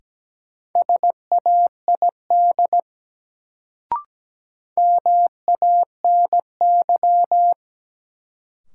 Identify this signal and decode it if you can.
Morse code, heard as a keyed tone, about 17 wpm, 700 Hz. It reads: SAID MANY
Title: Sentences from Top 100 Words - New Word Drill - 17wpm